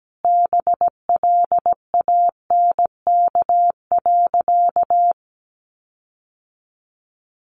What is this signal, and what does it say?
Morse code, heard as a keyed tone, about 17 wpm, 700 Hz. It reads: BLANK.
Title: Sentences from Top 1500 Words 17wpm